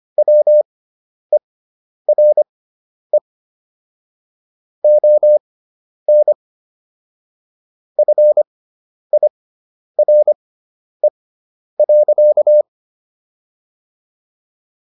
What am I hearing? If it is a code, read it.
Morse code, heard as a keyed tone, about 25 wpm, spaced as for 10 wpm, 600 Hz. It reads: WERE ON FIRE.